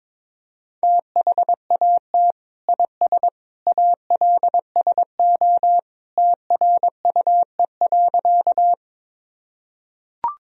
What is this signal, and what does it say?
Morse code, heard as a keyed tone, about 21 wpm, 700 Hz. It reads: THAT IS ALSO TRUE.
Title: Sentences from Top 1000 Words - Review Part 1 - 22wpm